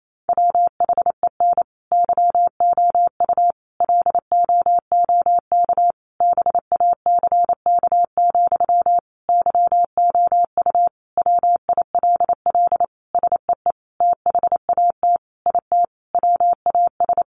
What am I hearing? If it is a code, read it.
Morse code, heard as a keyed tone, about 28 wpm, 700 Hz. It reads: WHEN YOU LOOK BACK, YOU WILL SEE THAT IT WAS
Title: Sentences from Top 800 Words 28wpm